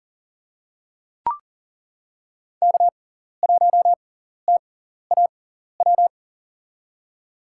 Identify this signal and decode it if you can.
Morse code, heard as a keyed tone, about 40 wpm, spaced as for 14 wpm, 700 Hz. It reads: K1TAW
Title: USA Call Signs - 5 Characters - 6x Character Spacing - 40wpm